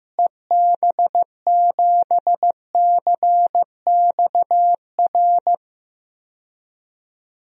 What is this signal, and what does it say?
Morse code, heard as a keyed tone, about 15 wpm, 700 Hz. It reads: EB7CXR
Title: DXCC-100 Call Signs - 6 Characters - Repeated 3x - 15wpm